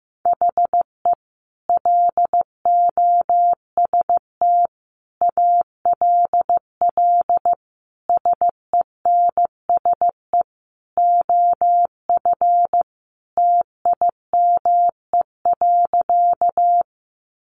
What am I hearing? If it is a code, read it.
Morse code, heard as a keyed tone, about 15 wpm, 700 Hz. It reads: HE LOST ALL SENSE OF TIME.
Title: Sentences from Top 1400 Words 15wpm